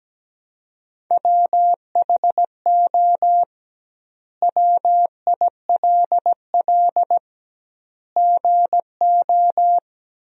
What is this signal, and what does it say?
Morse code, heard as a keyed tone, about 17 wpm, 700 Hz. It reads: WHO WILL GO
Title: 2x Word-Spacing - Sentences from Top 100 Words - 17wpm